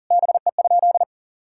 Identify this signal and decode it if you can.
Morse code, heard as a keyed tone, about 40 wpm, 700 Hz. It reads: BE?
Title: Sentences from Top 1000 Words - Review Part 5 - 40wpm